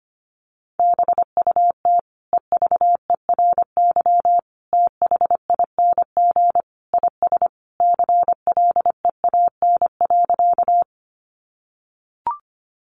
Morse code, heard as a keyed tone, about 25 wpm, 700 Hz. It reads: BUT EVERYTHING IS CLEAN.
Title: Sentences from Top 1100 Words 25wpm